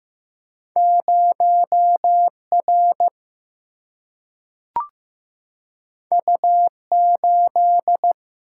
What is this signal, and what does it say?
Morse code, heard as a keyed tone, about 15 wpm, 700 Hz. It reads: 0R U8
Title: Single-Letter Single-Number 15wpm